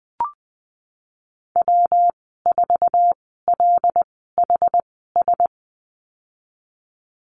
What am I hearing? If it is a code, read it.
Morse code, heard as a keyed tone, about 20 wpm, spaced as for 15 wpm, 700 Hz. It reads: W4LHS
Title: 173 Lesson 28g - Number 9 - Call Signs 20wpm